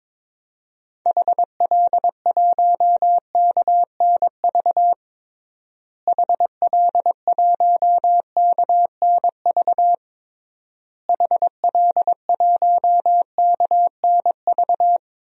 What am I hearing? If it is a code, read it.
Morse code, heard as a keyed tone, about 21 wpm, 700 Hz. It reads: HL1KNV HL1KNV HL1KNV
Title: DXCC-100 Call Signs - 6 Characters - Repeated 3x - 22wpm